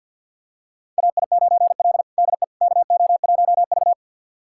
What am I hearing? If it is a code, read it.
Morse code, heard as a keyed tone, about 52 wpm, 700 Hz. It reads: AI9L DE KO1V
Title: USA Callsign DE Callsign - 4 Characters - 50wpm